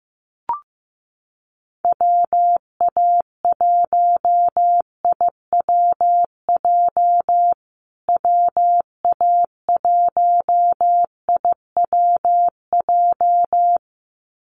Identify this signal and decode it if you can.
Morse code, heard as a keyed tone, about 15 wpm, 700 Hz. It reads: WA1IWJ WA1IWJ
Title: USA Call Signs Repeated 2x 15wpm